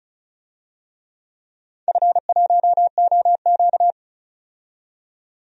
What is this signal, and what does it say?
Morse code, heard as a keyed tone, about 36 wpm, 700 Hz. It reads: F1OQ
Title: DXCC-100 Call Signs - 4 Characters - 35wpm